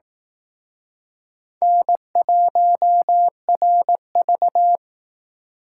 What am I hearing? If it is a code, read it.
Morse code, heard as a keyed tone, about 18 wpm, 700 Hz. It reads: N1RV
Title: Speed Racing - USA 1x2 Call Signs 15wpm